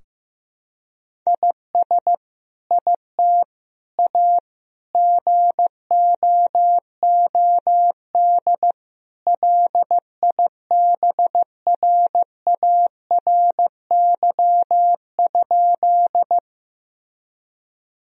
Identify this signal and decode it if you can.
Morse code, heard as a keyed tone, about 15 wpm, 700 Hz. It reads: IS IT A GOOD LIBRARY?
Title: Sentences from Top 1000 Words - Review Part 4 - 15wpm